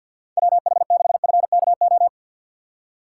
Morse code, heard as a keyed tone, about 53 wpm, 690 Hz. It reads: WH6FKO
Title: USA Call Signs - 6x Character Spacing - 50wpm